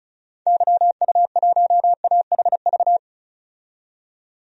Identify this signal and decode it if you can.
Morse code, heard as a keyed tone, about 35 wpm, 700 Hz. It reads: YU1AHV
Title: DXCC-100 Call Signs - 6 Characters - 35wpm